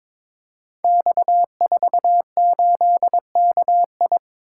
Morse code, heard as a keyed tone, about 22 wpm, 700 Hz. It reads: X48KI
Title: Code Groups - 5 Letter:Numbers - 22wpm